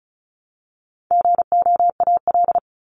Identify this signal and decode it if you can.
Morse code, heard as a keyed tone, about 34 wpm, 700 Hz. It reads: GOAL
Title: Top 751-1000 Words 35wpm